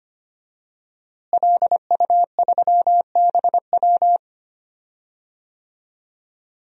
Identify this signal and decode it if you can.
Morse code, heard as a keyed tone, about 25 wpm, 700 Hz. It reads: LU3BW
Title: DXCC-100 Call Signs - 5 Characters - Repeated 3x - 25wpm